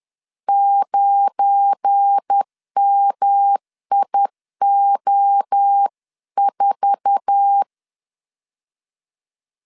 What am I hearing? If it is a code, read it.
Morse code, heard as a keyed tone, about 10 wpm, 800 Hz. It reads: 9MIO4
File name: LEZIONI / 8 YJ45 / 8 YJ4 10wpm 2633